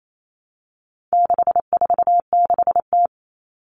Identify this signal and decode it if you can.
Morse code, heard as a keyed tone, about 28 wpm, 700 Hz. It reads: 646T
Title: All Cut Numbers - 4-Digit 28wpm